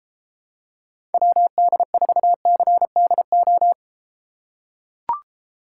Morse code, heard as a keyed tone, about 32 wpm, 700 Hz. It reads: WD4CDO